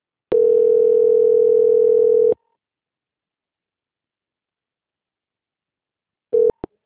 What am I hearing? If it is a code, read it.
Ringback tone (North America), 440 + 480 Hz.
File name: out- -2009- - - .8698